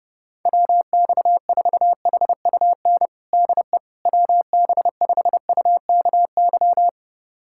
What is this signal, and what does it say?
Morse code, heard as a keyed tone, about 30 wpm, 700 Hz. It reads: WX4HUN DE WB5UKY
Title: USA Callsign DE Callsign - 6 Characters - 30wpm